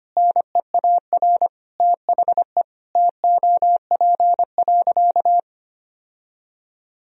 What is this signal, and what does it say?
Morse code, heard as a keyed tone, about 25 wpm, 700 Hz. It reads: NEAR THE TOP.
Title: Sentences from Top 1000 Words - Review Part 2 - 25wpm